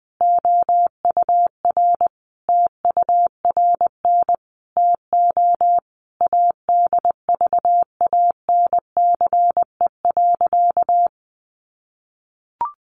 Morse code, heard as a keyed tone, about 20 wpm, 700 Hz. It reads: OUR TURN TO ADVANCE.
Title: Sentences from Top 1400 Words 20wpm